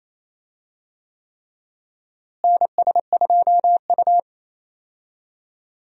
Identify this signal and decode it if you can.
Morse code, heard as a keyed tone, about 28 wpm, 700 Hz. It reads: NS2U
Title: USA Call Signs - 4 Characters - Repeated 3x - 28wpm